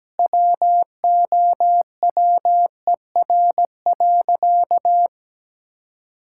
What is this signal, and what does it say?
Morse code, heard as a keyed tone, about 17 wpm, 700 Hz. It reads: WOWER.
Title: Pangram Challenge v2 - 17wpm